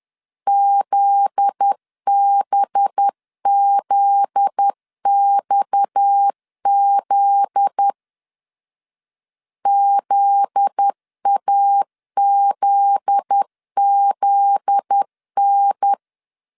Morse code, heard as a keyed tone, about 10 wpm, 800 Hz. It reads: ZBZXZ ZAZZN
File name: LEZIONI / 7 GF9Z / 7 GF9Z 10wpm 2611